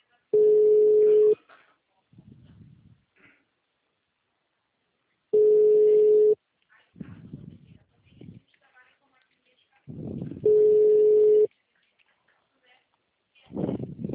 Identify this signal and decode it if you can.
Ringback tone (Europe, 425 Hz), with speech.